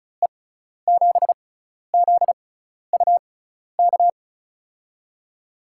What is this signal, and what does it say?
Morse code, heard as a keyed tone, about 36 wpm, spaced as for 12 wpm, 700 Hz. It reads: E7ZUK
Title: USA Call Signs - 6 Characters - 6x Character Spacing - 35wpm